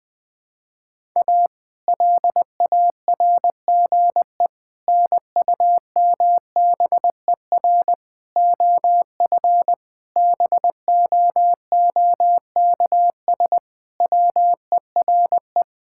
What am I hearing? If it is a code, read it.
Morse code, heard as a keyed tone, about 20 wpm, 700 Hz. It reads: A LARGE NUMBER OF BOOKS WERE
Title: Sentences with 4-Word Phrases - Spoken Beginning - 20wpm